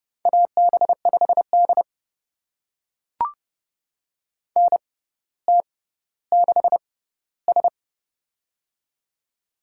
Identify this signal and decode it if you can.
Morse code, heard as a keyed tone, about 30 wpm, spaced as for 15 wpm, 700 Hz. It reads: AB5D NT6S
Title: USA Call Signs - 4 Characters - 6x Character Spacing - 30wpm